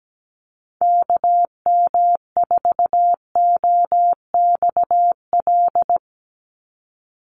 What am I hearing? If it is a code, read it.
Morse code, heard as a keyed tone, about 17 wpm, 700 Hz. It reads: KM4OXL
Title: USA Call Signs 17wpm